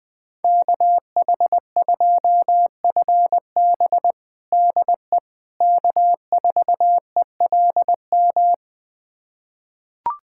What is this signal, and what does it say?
Morse code, heard as a keyed tone, about 21 wpm, 700 Hz. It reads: KH2FB DE K4ELM
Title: USA Callsign DE Callsign - 5 Characters - 20wpm